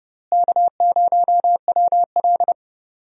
Morse code, heard as a keyed tone, about 29 wpm, 700 Hz. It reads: K0WL
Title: USA Call Signs - 4 Characters - Repeated 3x - 30wpm